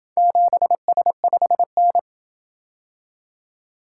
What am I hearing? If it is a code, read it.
Morse code, heard as a keyed tone, about 28 wpm, 700 Hz. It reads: 7S5N